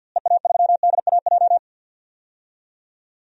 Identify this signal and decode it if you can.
Morse code, heard as a keyed tone, about 46 wpm, 700 Hz. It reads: EA3DAJ
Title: DXCC-100 Call Signs - 6 Characters - Repeated 3x - 50wpm